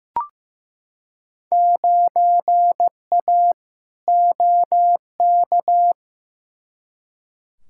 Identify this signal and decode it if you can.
Morse code, heard as a keyed tone, about 16 wpm, 700 Hz. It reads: 9A OK
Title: Field Day Exchanges Encore - 15wpm